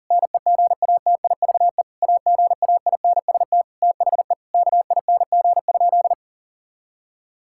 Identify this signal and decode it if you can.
Morse code, heard as a keyed tone, about 40 wpm, 700 Hz. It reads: NEGATIVE AGAINST THE KING?
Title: Sentences from Top 1000 Words - Review Part 6 - 40wpm